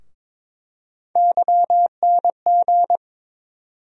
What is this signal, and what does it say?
Morse code, heard as a keyed tone, about 22 wpm, 700 Hz. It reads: YNG
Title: USA Call Sign Postfixes - 3 Letters 22wpm